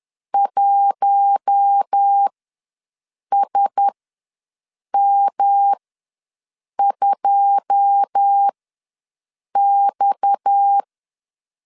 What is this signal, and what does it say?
Morse code, heard as a keyed tone, about 10 wpm, spaced as for 6 wpm, 800 Hz. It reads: 1SM2X
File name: LEZIONI / 9 286 ws 10wpm 4032